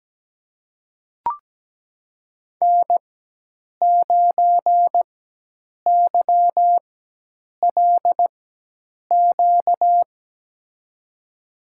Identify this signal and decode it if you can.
Morse code, heard as a keyed tone, about 17 wpm, spaced as for 8 wpm, 700 Hz. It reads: N9YLQ
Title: USA Call Signs - 5 Characters - 4x Character Spacing - 17wpm